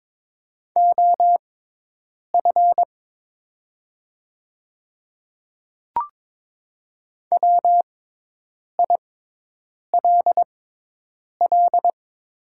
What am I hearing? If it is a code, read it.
Morse code, heard as a keyed tone, about 22 wpm, spaced as for 8 wpm, 700 Hz. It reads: OF WILL
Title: 6x Character Spacing - Top 100 Words - 22wpm